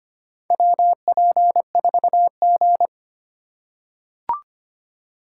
Morse code, heard as a keyed tone, about 25 wpm, 700 Hz. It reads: WP4G
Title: USA Call Signs - 4 Characters - 4x Character Spacing - 25wpm